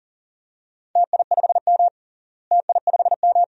Morse code, heard as a keyed tone, about 40 wpm, 700 Hz. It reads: TI5M TI5M
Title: DXCC-100 Call Signs - 4 Characters - Repeated 2x - 40wpm